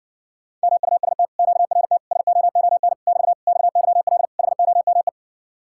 Morse code, heard as a keyed tone, about 48 wpm, 700 Hz. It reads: MUST QRT SOON <BT> XYL HOME
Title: Speed-Racing Common QSO Phrases - 40wpm